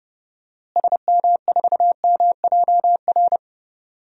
Morse code, heard as a keyed tone, about 29 wpm, 700 Hz. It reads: SM4MJR
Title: DXCC-100 Call Signs - 6 Characters - Repeated 2x - 30wpm